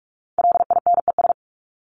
Morse code, heard as a keyed tone, about 44 wpm, 700 Hz. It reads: LINES
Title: ES:S Suffix - Top 100 Words with Base Words 45wpm